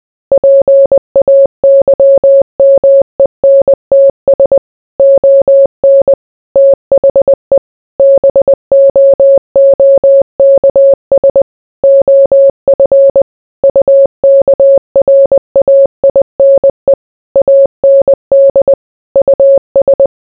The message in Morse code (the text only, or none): PAYMENTS ON THE BOOKS OF UKRAINE AND US